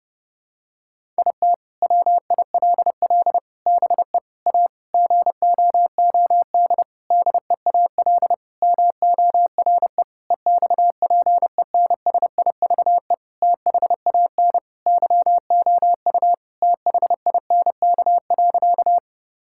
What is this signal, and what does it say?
Morse code, heard as a keyed tone, about 30 wpm, 700 Hz. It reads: IT WILL BE A GOOD DEAL MORE EXPENSIVE THAN YOU THINK.
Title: Sentences from Top 1000 Words - Review Part 6 - 30wpm